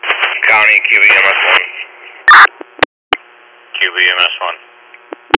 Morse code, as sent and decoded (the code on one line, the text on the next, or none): none